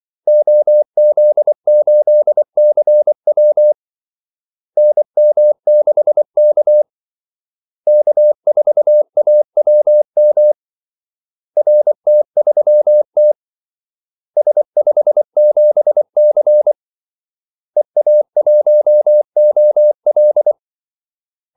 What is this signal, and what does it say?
Morse code, heard as a keyed tone, about 24 wpm, 600 Hz. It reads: OZ8CW NM6K K4AWM RT3T S57C EA1OL
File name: Callsigns-24wpm